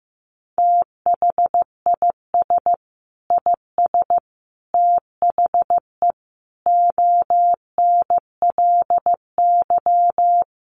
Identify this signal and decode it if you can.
Morse code, heard as a keyed tone, about 15 wpm, 700 Hz. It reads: THIS IS THE ONLY
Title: Sentences from Top 1400 Words 15wpm